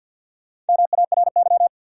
Morse code, heard as a keyed tone, about 51 wpm, 700 Hz. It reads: MARY
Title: USA Common Names - 50wpm